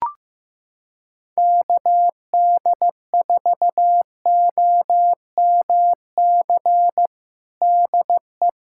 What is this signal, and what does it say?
Morse code, heard as a keyed tone, about 16 wpm, 700 Hz. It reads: KD4OMC DE
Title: USA Callsign DE Callsign - 6 Characters - 15wpm